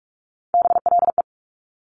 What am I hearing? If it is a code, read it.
Morse code, heard as a keyed tone, about 46 wpm, 690 Hz. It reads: BLE
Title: Top Three Letter Combinations in Words 45wpm